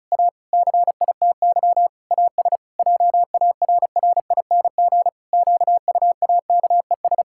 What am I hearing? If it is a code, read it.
Morse code, heard as a keyed tone, about 34 wpm, 700 Hz. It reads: A CITY AS JARRING QUAKES
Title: Pangram Challenge v2 - 35wpm